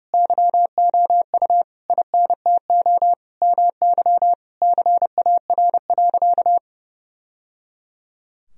Morse code, heard as a keyed tone, about 30 wpm, 700 Hz. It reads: YOU INTO MY CAR.
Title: Sentences from Top 1000 Words - Review Part 1 - 30wpm